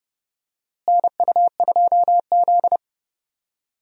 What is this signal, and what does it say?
Morse code, heard as a keyed tone, about 30 wpm, 700 Hz. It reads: NU2Z
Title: USA Call Signs - 4 Characters - Repeated 3x - 30wpm